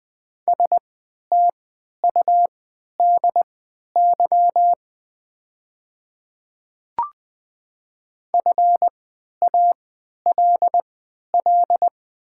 Morse code, heard as a keyed tone, about 20 wpm, spaced as for 11 wpm, 690 Hz. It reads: STUDY FALL
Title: 3x Character Spacing - Top 500 Words - 20wpm